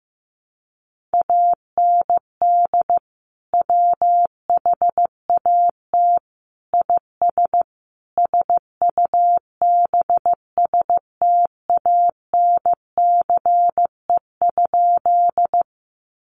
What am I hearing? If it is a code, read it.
Morse code, heard as a keyed tone, about 15 wpm, 700 Hz. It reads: AND WHAT IS SUBSTANCE?
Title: Sentences from Top 1500 Words 15wpm